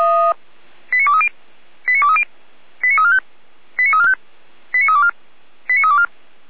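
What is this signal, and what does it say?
Five-tone selective call (ZVEI), heard as quick runs of five tones, tones of about 70 ms. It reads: 89230 89230 89345 89345 89324 89324
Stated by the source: ZVEI five-tone selective call 89001 received on an SDR.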